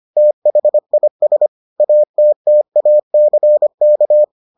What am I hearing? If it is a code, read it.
Morse code, heard as a keyed tone, about 25 wpm, 600 Hz. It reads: THIS ATTACK